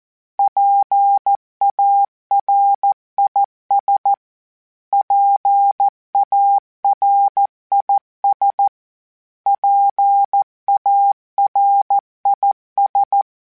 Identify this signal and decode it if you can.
Morse code, heard as a keyed tone, about 14 wpm, 800 Hz. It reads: PARIS PARIS PARIS
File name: PARIS Calibration / PARIS mp3s / PARIS@13